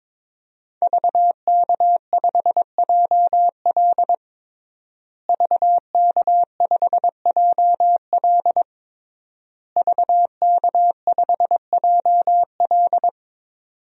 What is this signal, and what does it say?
Morse code, heard as a keyed tone, about 22 wpm, 700 Hz. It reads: VK5JL VK5JL VK5JL
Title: DXCC-100 Call Signs - 5 Characters - Repeated 3x - 22wpm